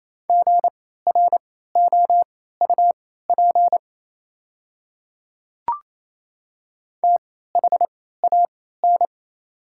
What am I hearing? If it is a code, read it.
Morse code, heard as a keyed tone, about 28 wpm, spaced as for 16 wpm, 690 Hz. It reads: GROUP THAN
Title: 3x Character Spacing - Top 500 Words - 28wpm